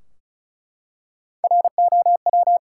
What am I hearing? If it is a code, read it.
Morse code, heard as a keyed tone, about 34 wpm, 700 Hz. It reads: ROW
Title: Three Letter Words - 35wpm